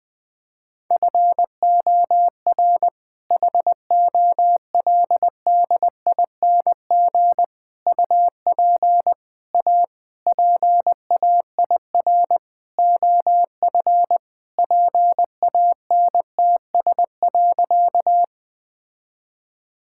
Morse code, heard as a keyed tone, about 20 wpm, 700 Hz. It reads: FOR HOLDING UP A PAIR OF PANTS.
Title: All-Age Appropriate Jokes - 20wpm